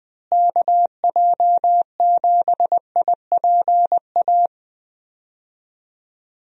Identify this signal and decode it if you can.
Morse code, heard as a keyed tone, about 20 wpm, 700 Hz. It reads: KJ7IPA
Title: USA Call Signs - 6 Characters - Repeated 2x - 20wpm